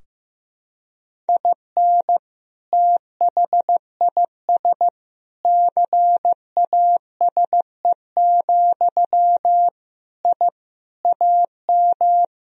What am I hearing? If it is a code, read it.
Morse code, heard as a keyed tone, about 15 wpm, 700 Hz. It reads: IN THIS CASE, I AM
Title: Sentences from Top 1000 Words - Review Part 4 - 15wpm